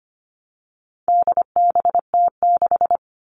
Morse code, heard as a keyed tone, about 25 wpm, 700 Hz. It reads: DBT6
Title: All Cut Numbers - 4-Digit 25wpm